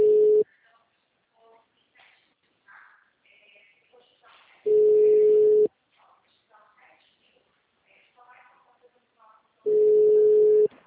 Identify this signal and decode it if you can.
Ringback tone (Europe, 425 Hz), with speech.